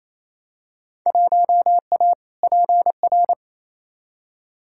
Morse code, heard as a keyed tone, about 28 wpm, 700 Hz. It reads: 1A PR
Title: Field Day Exchanges Encore - 28wpm